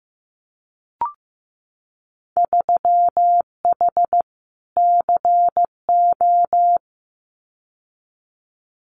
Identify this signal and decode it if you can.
Morse code, heard as a keyed tone, about 15 wpm, 700 Hz. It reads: 3H CO
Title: Winter Field Day Exchanges 15wpm